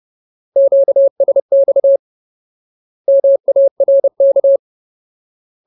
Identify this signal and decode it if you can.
Morse code, heard as a keyed tone, about 30 wpm, 550 Hz. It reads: QSX MARK
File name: Ultimate Word List / 30WPM Ultimate